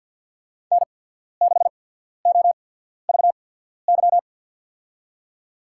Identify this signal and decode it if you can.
Morse code, heard as a keyed tone, about 53 wpm, spaced as for 14 wpm, 700 Hz. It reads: N6OVY
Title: USA Call Signs - 5 Characters - 8x Character Spacing - 50wpm